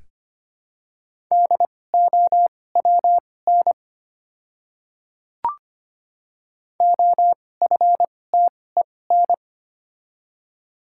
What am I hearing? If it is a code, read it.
Morse code, heard as a keyed tone, about 25 wpm, spaced as for 18 wpm, 700 Hz. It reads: DOWN OFTEN
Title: 2x Character Spacing - Top 300 Words - 25wpm